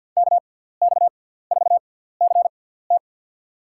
Morse code, heard as a keyed tone, about 53 wpm, spaced as for 17 wpm, 700 Hz. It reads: KX4CT
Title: USA Call Signs - 6x Character Spacing - 50wpm